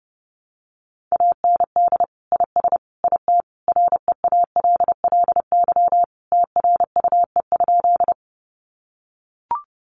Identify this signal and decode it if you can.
Morse code, heard as a keyed tone, about 30 wpm, 700 Hz. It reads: AND IS IT REALLY TRUE?
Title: Sentences from Top 600 Words 30wpm